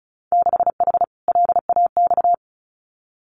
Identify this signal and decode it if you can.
Morse code, heard as a keyed tone, about 36 wpm, 700 Hz. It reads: 6H LAX
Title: Winter Field Day Exchanges Encore 35wpm